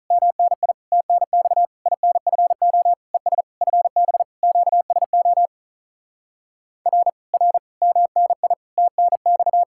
Morse code, heard as a keyed tone, about 48 wpm, 700 Hz. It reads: MNI TNX INFO ES FB QSO R R MNI TNX